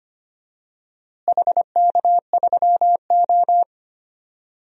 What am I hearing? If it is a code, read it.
Morse code, heard as a keyed tone, about 25 wpm, 700 Hz. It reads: HK3O
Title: DXCC-100 Call Signs - 4 Characters - Repeated 2x - 25wpm